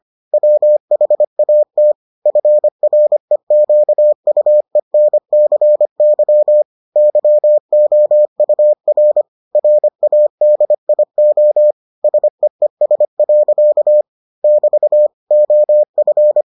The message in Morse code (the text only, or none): WHAT FREQUENCY YOUR RADIO SEES. <BT> OF